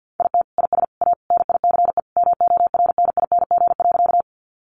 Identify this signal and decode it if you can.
Morse code, heard as a keyed tone, about 49 wpm, 700 Hz. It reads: IT IS A NICE MORNING.
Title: Sentences from Top 1100 Words 50wpm